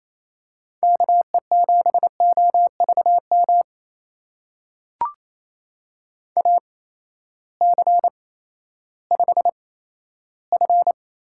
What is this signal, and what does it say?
Morse code, heard as a keyed tone, about 28 wpm, spaced as for 14 wpm, 700 Hz. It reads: KE7OVM AC5F
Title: USA Call Signs - 8x Character Spacing - 28wpm